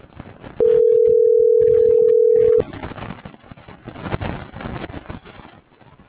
Ringback tone (North America), 440 + 480 Hz.